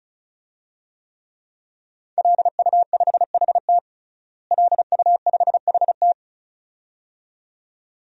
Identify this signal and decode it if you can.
Morse code, heard as a keyed tone, about 36 wpm, 700 Hz. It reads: LU5HT LU5HT
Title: DXCC-100 Call Signs - 5 Characters - Repeated 2x - 35wpm